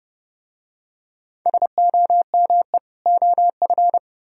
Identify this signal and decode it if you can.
Morse code, heard as a keyed tone, about 30 wpm, 700 Hz. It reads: SOME OF